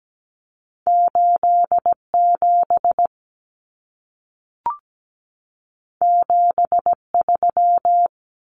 Morse code, heard as a keyed tone, about 17 wpm, 700 Hz. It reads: 87 73
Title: Common Cut Numbers - 2-Digit 17wpm